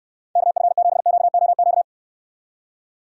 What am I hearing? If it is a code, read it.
Morse code, heard as a keyed tone, about 71 wpm, 700 Hz. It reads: KF7QOQ